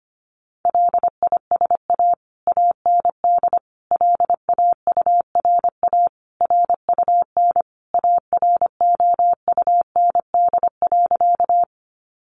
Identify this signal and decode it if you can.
Morse code, heard as a keyed tone, about 25 wpm, 700 Hz. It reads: LISA AND LAURA RUN AROUND.
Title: Additional Easy Phrases 25wpm